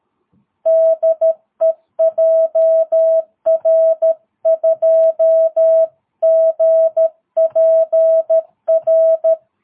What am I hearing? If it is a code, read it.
Morse code, heard as a keyed tone, about 13 wpm, 660 Hz. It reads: DE JR2GPR
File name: jh8cbh / 2022cw